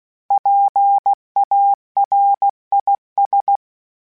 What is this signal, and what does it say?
Morse code, heard as a keyed tone, about 16 wpm, 800 Hz. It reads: PARIS
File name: PARIS Calibration / PARIS mp3s / PARIS@15